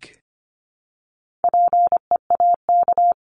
Morse code, heard as a keyed tone, about 25 wpm, 700 Hz. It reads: PEAK
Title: Top 1751-2000 Words 25wpm